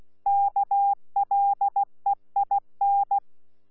Morse code, heard as a keyed tone, about 16 wpm, 800 Hz. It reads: KLEIN
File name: Z PI4VRZ / PI4VRZ 14-jun-2025